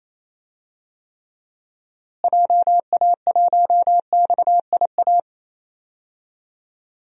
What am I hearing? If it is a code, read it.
Morse code, heard as a keyed tone, about 28 wpm, 700 Hz. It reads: JA1XIA
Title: DXCC-100 Call Signs - 6 Characters - Repeated 2x - 28wpm